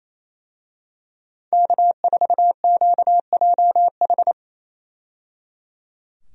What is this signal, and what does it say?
Morse code, heard as a keyed tone, about 28 wpm, 700 Hz. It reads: K4QJH
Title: USA Call Signs - 5 Characters - 28wpm